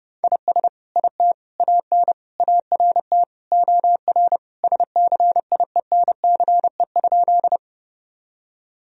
Morse code, heard as a keyed tone, about 30 wpm, 700 Hz. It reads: IS IT AN ART OR SCIENCE?
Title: Sentences from Top 1000 Words - Review Part 3 - 30wpm